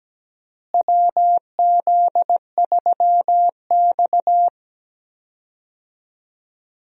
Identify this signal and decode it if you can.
Morse code, heard as a keyed tone, about 17 wpm, 700 Hz. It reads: WZ3X
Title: USA Call Signs - 4 Characters - Repeated 2x - 17wpm